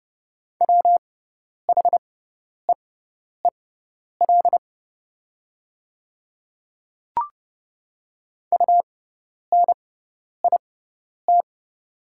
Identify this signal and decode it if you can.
Morse code, heard as a keyed tone, about 30 wpm, spaced as for 10 wpm, 700 Hz. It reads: WHEEL UNIT
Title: 6x Character Spacing - Top 500 Words - 30wpm